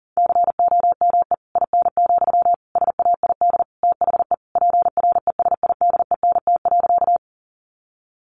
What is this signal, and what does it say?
Morse code, heard as a keyed tone, about 40 wpm, 700 Hz. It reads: COME IN, SAID THE PRESIDENT.
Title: Sentences from Top 1000 Words 40wpm